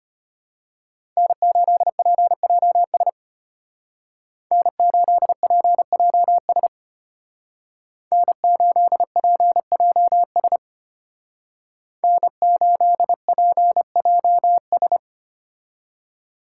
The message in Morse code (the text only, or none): N8PJS N8PJS N8PJS N8PJS